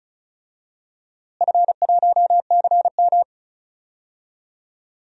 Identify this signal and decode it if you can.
Morse code, heard as a keyed tone, about 36 wpm, 700 Hz. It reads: F1CM
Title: DXCC-100 Call Signs - 4 Characters - Repeated 2x - 35wpm